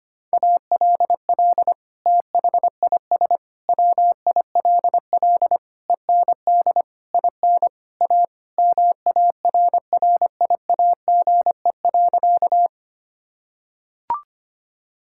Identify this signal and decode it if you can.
Morse code, heard as a keyed tone, about 25 wpm, 700 Hz. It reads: ALL THIS WILL END IN A MARRIAGE.
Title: Sentences from Top 1000 Words - Review Part 5 - 25wpm